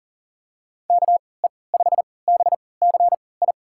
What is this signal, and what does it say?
Morse code, heard as a keyed tone, about 40 wpm, spaced as for 23 wpm, 700 Hz. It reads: KE5BCI
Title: USA Call Signs - 6 Characters - 3x Character Spacing - 40wpm